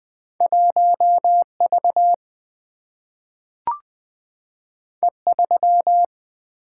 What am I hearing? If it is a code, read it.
Morse code, heard as a keyed tone, about 20 wpm, 700 Hz. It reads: 1V E3
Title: Single-Letter Single-Number 20wpm